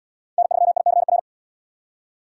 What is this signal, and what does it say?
Morse code, heard as a keyed tone, about 77 wpm, 700 Hz. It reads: N3ICR